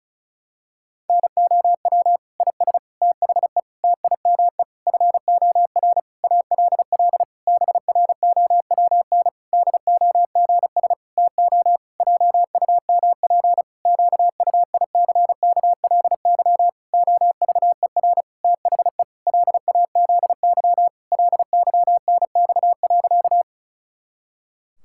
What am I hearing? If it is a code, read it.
Morse code, heard as a keyed tone, about 36 wpm, 700 Hz. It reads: NOW IS THE TIME FOR ALL BROWN DOGS TO JUMP QUICKLY OVER THE LAZY LYNX.
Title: Pangram Challenge v2 - 35wpm